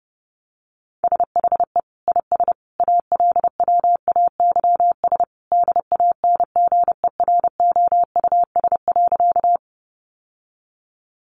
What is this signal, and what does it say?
Morse code, heard as a keyed tone, about 28 wpm, 700 Hz. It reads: SHE IS ALWAYS DANGEROUS.
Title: Sentences from Top 1200 Words 30wpm